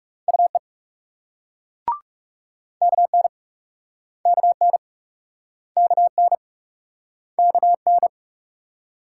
Morse code, heard as a keyed tone, about 42 wpm, 700 Hz. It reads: UE KN KN KN KN